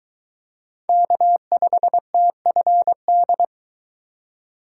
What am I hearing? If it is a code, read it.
Morse code, heard as a keyed tone, about 22 wpm, 700 Hz. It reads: K5TFD